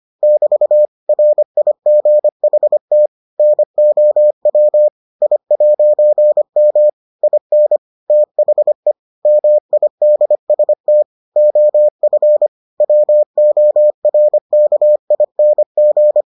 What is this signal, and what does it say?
Morse code, heard as a keyed tone, about 25 wpm, 600 Hz. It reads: <BT> RIGHT NOW I'M IN THE MIDST OF WORKING